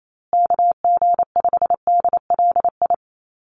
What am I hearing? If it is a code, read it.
Morse code, heard as a keyed tone, about 28 wpm, 700 Hz. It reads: KG5DLI
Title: USA Call Signs Repeated 3x 28wpm